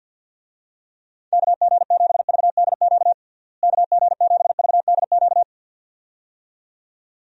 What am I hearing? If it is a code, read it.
Morse code, heard as a keyed tone, about 51 wpm, 700 Hz. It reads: KG7VDQ KG7VDQ
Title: USA Call Signs - 6 Characters - Repeated 2x - 50wpm